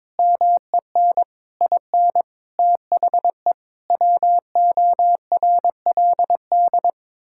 Morse code, heard as a keyed tone, about 22 wpm, 700 Hz. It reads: MEN IN THE WORLD
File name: practice / 22wpm / Sets of 8 Words - 22wpm